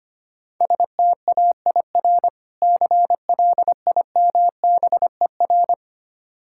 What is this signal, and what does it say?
Morse code, heard as a keyed tone, about 25 wpm, 700 Hz. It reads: STAIR CLIMBER
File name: practice / 25wpm / Difficult Words - Speed-Racing with Hints - 25wpm